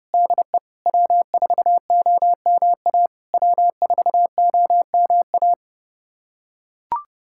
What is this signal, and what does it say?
Morse code, heard as a keyed tone, about 31 wpm, 690 Hz. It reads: DE W4OMA W4OMA
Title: SOTA QSO Practice 30wpm